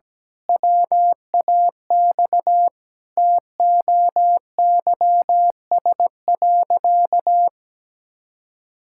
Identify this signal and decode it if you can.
Morse code, heard as a keyed tone, about 17 wpm, 700 Hz. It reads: WAX TOYS.
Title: Pangram Challenge v2 - 17wpm